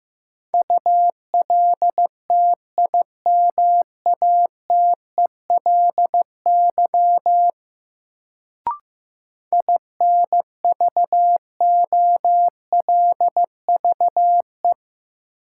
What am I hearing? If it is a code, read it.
Morse code, heard as a keyed tone, about 15 wpm, 700 Hz. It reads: ULTIMATELY INVOLVE
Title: Sentences from Top 1200 Words - New Word Drill - 15wpm